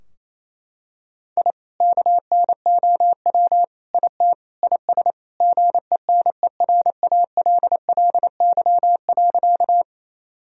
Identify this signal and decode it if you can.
Morse code, heard as a keyed tone, about 27 wpm, 700 Hz. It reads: I KNOW IT IS GENERALLY.
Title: Sentences from Top 1000 Words - Review Part 4 - 28wpm